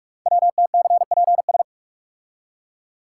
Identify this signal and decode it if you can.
Morse code, heard as a keyed tone, about 44 wpm, 700 Hz. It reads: WTCPS
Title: Code Groups - 5 Letters - 45wpm